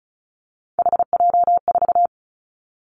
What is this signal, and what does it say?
Morse code, heard as a keyed tone, about 34 wpm, 700 Hz. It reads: HJ4
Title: Callsign Prefixes for Top 100 DXCC - 35wpm